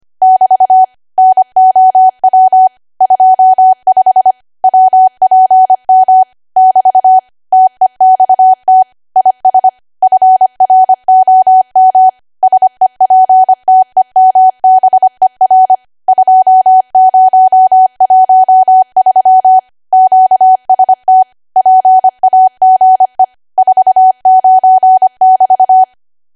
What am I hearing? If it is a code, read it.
Morse code, heard as a keyed tone, about 25 wpm, 750 Hz. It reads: <BT> NOW 25 WPM <BT> TEXT IS FROM SEPTEMBER 2013 QST PAGE 49<BT>